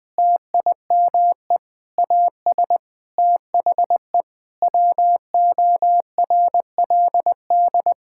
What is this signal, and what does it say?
Morse code, heard as a keyed tone, about 20 wpm, 700 Hz. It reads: TIME AS THE WORLD
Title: Sentences with 9-Word Phrases - 20wpm